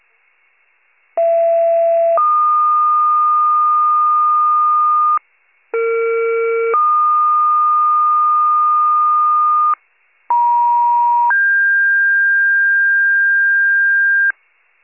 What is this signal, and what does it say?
Two-tone page, heard as a one-second tone, then a three-second tone, about 670 then 1220 Hz, about 460 then 1220 Hz, about 950 then 1600 Hz. It is heard from 1.2 to 14.3 s.